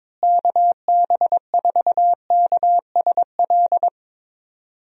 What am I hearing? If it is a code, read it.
Morse code, heard as a keyed tone, about 21 wpm, 700 Hz. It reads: KB4KSL
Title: USA Call Signs - 6 Characters - Repeated 3x - 22wpm